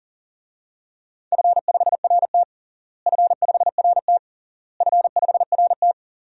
Morse code, heard as a keyed tone, about 37 wpm, 700 Hz. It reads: F5RT F5RT F5RT
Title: DXCC-100 Call Signs - 4 Characters - Repeated 3x - 40wpm